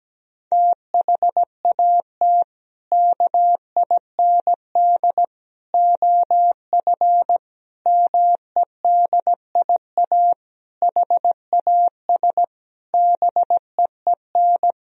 Morse code, heard as a keyed tone, about 17 wpm, 700 Hz. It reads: THAT KIND OF MEDIA HAS BEEN
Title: Sentences from Top 1000 Words - Review Part 4 - 17wpm